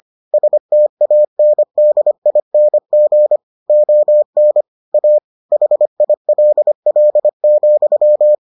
Morse code, heard as a keyed tone, about 25 wpm, 600 Hz. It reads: STANDING ON A HILL,